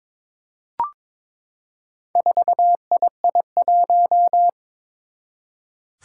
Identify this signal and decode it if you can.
Morse code, heard as a keyed tone, about 22 wpm, 700 Hz. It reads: 4II1
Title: Four Challenging Letter-Number Combinations 22wpm